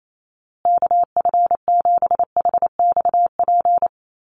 Morse code, heard as a keyed tone, about 28 wpm, 700 Hz. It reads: KF7HXP
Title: USA Call Signs Repeated 3x 28wpm